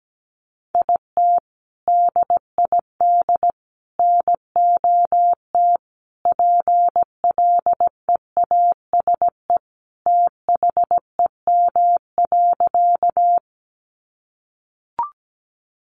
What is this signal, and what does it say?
Morse code, heard as a keyed tone, about 17 wpm, 700 Hz. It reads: IT DID NOT PLEASE THEM.
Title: Sentences from Top 1500 Words 17wpm